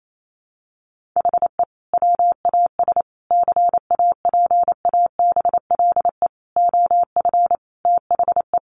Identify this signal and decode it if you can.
Morse code, heard as a keyed tone, about 28 wpm, 700 Hz. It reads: HE WAS CAPABLE OF THE
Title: Sentences from Top 1200 Words 28wpm